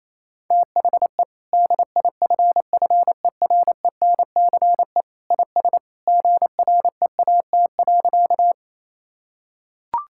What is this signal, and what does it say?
Morse code, heard as a keyed tone, about 28 wpm, 700 Hz. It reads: THE DIFFERENCE IS GREAT.
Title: Sentences from Top 1000 Words - Review Part 5 - 28wpm